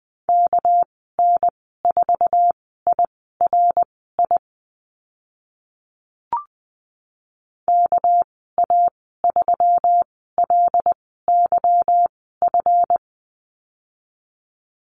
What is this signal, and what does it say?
Morse code, heard as a keyed tone, about 20 wpm, spaced as for 14 wpm, 700 Hz. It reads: KN4IRI KA3LYF
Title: 207 Lesson 33g - Letter j - Call Signs 20wpm